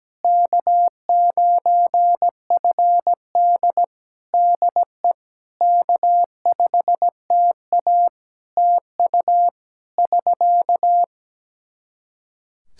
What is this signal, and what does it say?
Morse code, heard as a keyed tone, about 17 wpm, 700 Hz. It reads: K9FD DE K5TA TU <SK>
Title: K1USN SST QSO Practice 17wpm